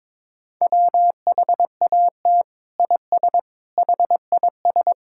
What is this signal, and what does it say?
Morse code, heard as a keyed tone, about 22 wpm, 700 Hz. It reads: WHAT IS HIS
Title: Sentences from Top 1000 Words - Review Part 5 - 22wpm